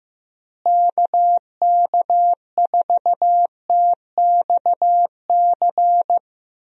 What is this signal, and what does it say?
Morse code, heard as a keyed tone, about 15 wpm, 700 Hz. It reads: KK4TXC
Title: USA Call Signs - 6 Characters - Repeated 3x - 15wpm